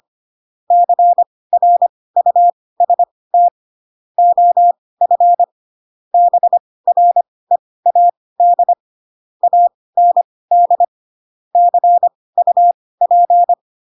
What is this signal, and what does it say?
Morse code, heard as a keyed tone, about 25 wpm, spaced as for 18 wpm, 700 Hz. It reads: CRUST OF BREAD AND CUP
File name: Pinocchio / Pinocchio ch2 12 25 ik1hge